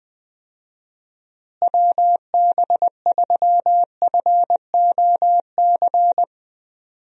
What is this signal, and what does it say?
Morse code, heard as a keyed tone, about 20 wpm, 700 Hz. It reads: WB3FOC
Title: USA Call Signs - 6 Characters - Repeated 2x - 20wpm